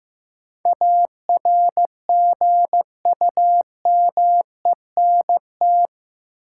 Morse code, heard as a keyed tone, about 15 wpm, 700 Hz. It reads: ARGUMENT
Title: Sentences from Top 1400 Words - New Word Drill Alt - 15wpm